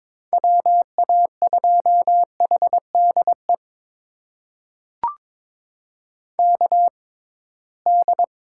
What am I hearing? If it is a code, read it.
Morse code, heard as a keyed tone, about 22 wpm, spaced as for 17 wpm, 700 Hz. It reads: WA2HDE KD
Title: USA Call Signs - 6 Characters - 6x Character Spacing - 22wpm